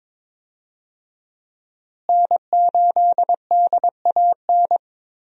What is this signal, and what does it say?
Morse code, heard as a keyed tone, about 22 wpm, 700 Hz. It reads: N8DAN
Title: USA Call Signs - 5 Characters - Repeated 2x - 22wpm